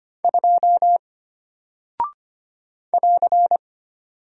Morse code, heard as a keyed tone, about 25 wpm, 700 Hz. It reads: 2 <AR>
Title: Single Character (Letter:Number:Punctuation:Prosign) Alt 25wpm